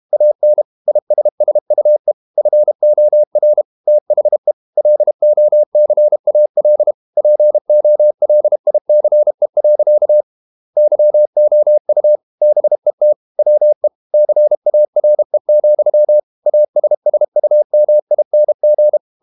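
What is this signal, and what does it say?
Morse code, heard as a keyed tone, about 32 wpm, 600 Hz. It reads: AN ISSUE FOR THE LOCAL POLICE. YOU BET WE CARE, ASSUMING
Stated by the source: Morse code practice at 32 wpm from CW med Gnister, reading Ken Follett's novel Never in English.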